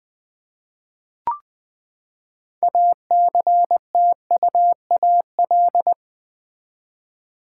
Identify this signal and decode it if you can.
Morse code, heard as a keyed tone, about 20 wpm, 700 Hz. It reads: ACTUAL
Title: Six Letter Words - 20wpm